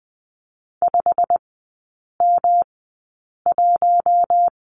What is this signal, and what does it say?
Morse code, heard as a keyed tone, about 20 wpm, spaced as for 8 wpm, 700 Hz. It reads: 5M1
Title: 121 Lesson 20e - Number 3 - Three Characters 20wpm